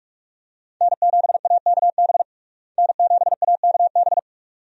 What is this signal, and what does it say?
Morse code, heard as a keyed tone, about 42 wpm, 700 Hz. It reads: N7AKB N7AKB
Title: USA Call Signs - 5 Characters - Repeated 2x - 45wpm